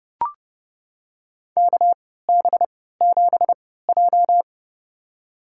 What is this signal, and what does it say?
Morse code, heard as a keyed tone, about 30 wpm, spaced as for 17 wpm, 700 Hz. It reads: KB7J
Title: USA Call Signs - 4 Characters - 3x Character Spacing - 30wpm